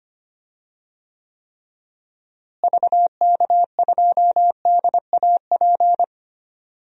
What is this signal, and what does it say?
Morse code, heard as a keyed tone, about 25 wpm, 700 Hz. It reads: VK2DAP
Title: DXCC-100 Call Signs - 6 Characters - Repeated 3x - 25wpm